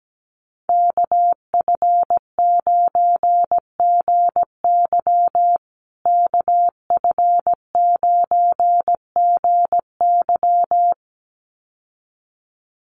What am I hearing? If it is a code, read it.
Morse code, heard as a keyed tone, about 17 wpm, 700 Hz. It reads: KF9GY KF9GY
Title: USA Call Signs Repeated 2x 17wpm